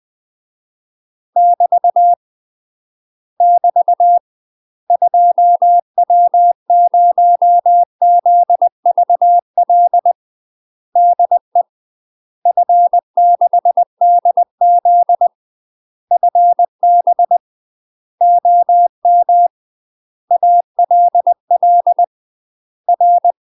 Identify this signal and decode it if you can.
Morse code, heard as a keyed tone, about 20 wpm, 700 Hz. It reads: <BT> <BT> 2W0ZVL DE F6DZ FB OM ALL R
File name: UK MASTER QSOs TRAINING / 20WPM / UK MASTER QSOs 0039